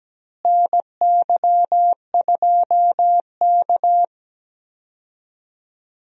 Morse code, heard as a keyed tone, about 17 wpm, 700 Hz. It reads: NY2K
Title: USA Call Signs - 4 Characters - Repeated 3x - 17wpm